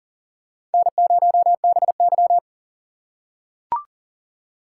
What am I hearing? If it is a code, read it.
Morse code, heard as a keyed tone, about 40 wpm, 700 Hz. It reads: N0BY
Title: USA Call Signs - 4 Characters - 6x Character Spacing - 40wpm